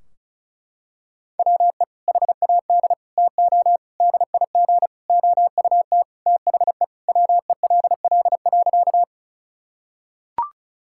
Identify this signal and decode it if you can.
Morse code, heard as a keyed tone, about 34 wpm, 700 Hz. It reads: WE HAD TO DIG OUT THE WELL.
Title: Sentences from Top 1600 Words 35wpm